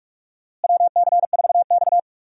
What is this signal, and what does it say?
Morse code, heard as a keyed tone, about 46 wpm, 700 Hz. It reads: WC4X
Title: USA Call Signs - 4 Characters - Repeated 2x - 45wpm